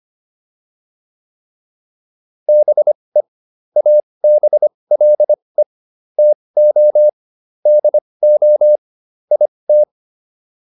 Morse code, heard as a keyed tone, about 25 wpm, spaced as for 20 wpm, 600 Hz. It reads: BE ABLE TO DO IT